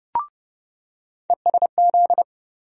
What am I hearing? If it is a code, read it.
Morse code, heard as a keyed tone, about 32 wpm, 700 Hz. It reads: ESZ
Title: USA Call Sign Postfixes - 3 Letters 30wpm